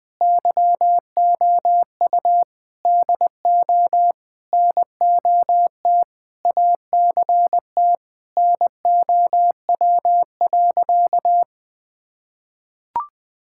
Morse code, heard as a keyed tone, about 21 wpm, 700 Hz. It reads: YOU DO NOT ACT NOW.
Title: Sentences with 8-Word Phrases - 20wpm